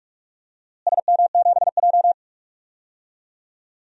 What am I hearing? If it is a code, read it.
Morse code, heard as a keyed tone, about 41 wpm, 700 Hz. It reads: SM7J
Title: DXCC-100 Call Signs - 4 Characters - Repeated 2x - 45wpm